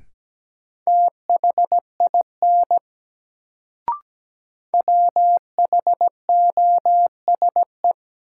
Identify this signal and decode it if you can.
Morse code, heard as a keyed tone, about 17 wpm, 700 Hz. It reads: THIN WHOSE
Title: Sentences from Top 1200 Words - New Word Drill Alt - 17wpm